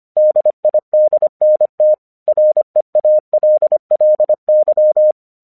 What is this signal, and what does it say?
Morse code, heard as a keyed tone, about 24 wpm, 610 Hz. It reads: DIDNT REALLY
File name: 1 English 25wpm / AQC19